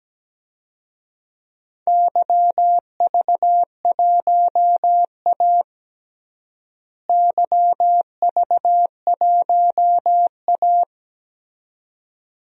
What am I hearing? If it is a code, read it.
Morse code, heard as a keyed tone, about 17 wpm, 700 Hz. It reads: YV1A YV1A
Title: DXCC-100 Call Signs - 4 Characters - Repeated 2x - 17wpm